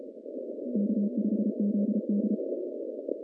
Morse code, heard as a keyed tone, about 33 wpm, 210 Hz. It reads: K5ZD
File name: 2015 cq ww cw / 2015 cqwww k5tr - 0-20